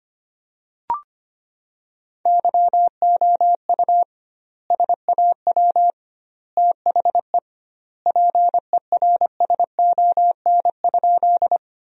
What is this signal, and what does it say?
Morse code, heard as a keyed tone, about 25 wpm, 700 Hz. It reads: YOU SAW THE PERSON?
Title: 2x Word-Spacing - Sentences from Top 400 Words - 25wpm